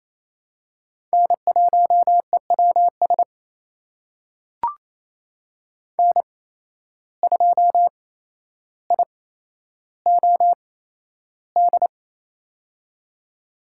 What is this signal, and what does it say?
Morse code, heard as a keyed tone, about 28 wpm, spaced as for 12 wpm, 700 Hz. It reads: N1EWS N2IOD
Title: USA Call Signs - 5 Characters - 8x Character Spacing - 28wpm